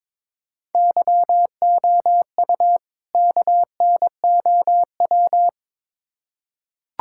Morse code, heard as a keyed tone, about 22 wpm, 700 Hz. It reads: YOU KNOW
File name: Build A Phrase - 2-Word Phrases - 22wpm